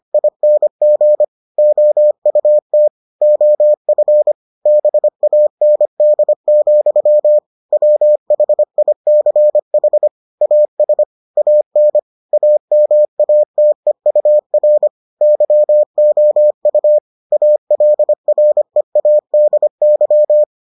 Morse code, heard as a keyed tone, about 25 wpm, 600 Hz. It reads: ING OUT OF BAND, WHICH AS AN AMATEUR YOU ALREADY